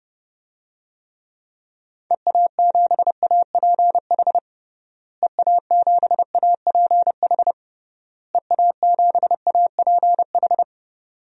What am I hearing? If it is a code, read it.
Morse code, heard as a keyed tone, about 32 wpm, 700 Hz. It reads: EA7APH EA7APH EA7APH
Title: DXCC-100 Call Signs - 6 Characters - Repeated 3x - 30wpm